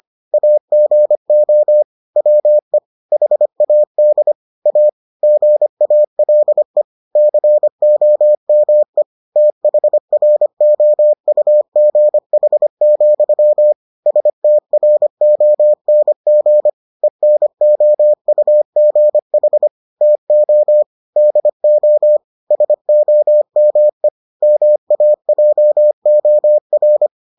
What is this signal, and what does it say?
Morse code, heard as a keyed tone, about 25 wpm, 600 Hz. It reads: AGO WE HAD A GALE COME THROUGH, STRONG ENOUGH TO DO SOME MAJOR